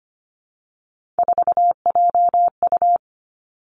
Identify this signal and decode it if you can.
Morse code, heard as a keyed tone, about 25 wpm, 700 Hz. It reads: 4JU
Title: Three Challenging Letter-Number Combinations 25wpm